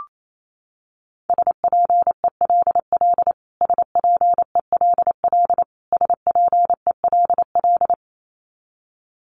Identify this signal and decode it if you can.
Morse code, heard as a keyed tone, about 28 wpm, 700 Hz. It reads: SPELL SPELL SPELL
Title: Top 500 Words Repeated 28wpm